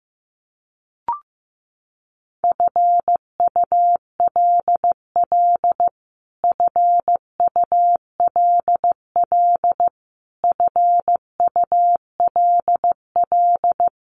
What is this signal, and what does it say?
Morse code, heard as a keyed tone, about 15 wpm, 700 Hz. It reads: FULL FULL FULL
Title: Top 500 Words Repeated 15wpm